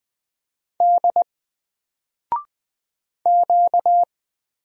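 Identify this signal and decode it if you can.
Morse code, heard as a keyed tone, about 20 wpm, 700 Hz. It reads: D Q
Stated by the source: Single letter and number Morse drill at 20 wpm, from N4LCD.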